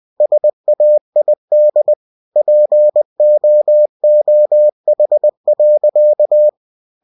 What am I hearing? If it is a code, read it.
Morse code, heard as a keyed tone, about 20 wpm, 600 Hz. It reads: SAID POOH.